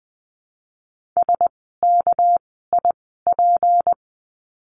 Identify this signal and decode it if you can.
Morse code, heard as a keyed tone, about 20 wpm, spaced as for 15 wpm, 700 Hz. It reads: SKIP
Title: 199 Lesson 32f - Letter k - Words 20wpm